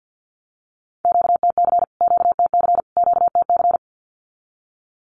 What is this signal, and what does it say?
Morse code, heard as a keyed tone, about 50 wpm, 700 Hz. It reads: QTC QTC QTC
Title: Q-Codes Repeated 3x 50wpm